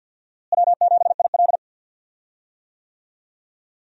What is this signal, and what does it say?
Morse code, heard as a keyed tone, about 50 wpm, 700 Hz. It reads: W7IL
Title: USA Call Signs - 4 Characters - Repeated 2x - 50wpm